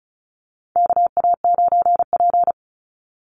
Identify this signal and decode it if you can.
Morse code, heard as a keyed tone, about 34 wpm, 700 Hz. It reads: KA9P
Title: Call Signs Repeated 2x - 35wpm